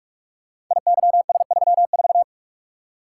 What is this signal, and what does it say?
Morse code, heard as a keyed tone, about 44 wpm, 700 Hz. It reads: IYS34
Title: Code Groups - 5 Letter:Numbers - 45wpm